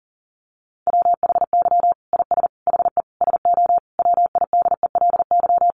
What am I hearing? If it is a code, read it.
Morse code, heard as a keyed tone, about 37 wpm, 700 Hz. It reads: WHY IS HE SO WIDELY
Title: Sentences from Top 1200 Words 40wpm